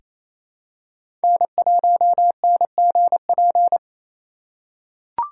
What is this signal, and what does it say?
Morse code, heard as a keyed tone, about 28 wpm, 700 Hz. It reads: N1NGP
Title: USA Call Signs - 4x Character Spacing - 28wpm